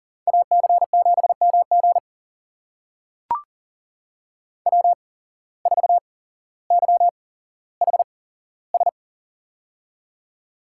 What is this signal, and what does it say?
Morse code, heard as a keyed tone, about 41 wpm, spaced as for 17 wpm, 700 Hz. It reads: AC7MG W4YHS
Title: USA Call Signs - 8x Character Spacing - 40wpm